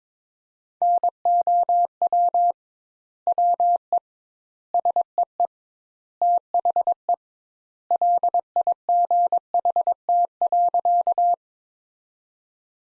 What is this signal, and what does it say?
Morse code, heard as a keyed tone, about 22 wpm, 700 Hz. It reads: NOW WE SEE THE LIGHT.
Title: Sentences Top 200 Words - 22wpm